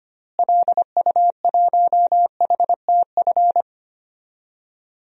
Morse code, heard as a keyed tone, about 25 wpm, 700 Hz. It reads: LU1HTF
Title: DXCC-100 Call Signs - 6 Characters - Repeated 2x - 25wpm